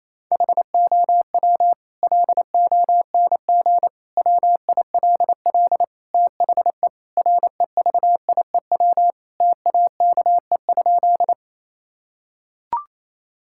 Morse code, heard as a keyed tone, about 28 wpm, 700 Hz. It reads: HOW LONG WILL THE REVIEW TAKE?
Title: Sentences from Top 1000 Words - Review Part 6 - 28wpm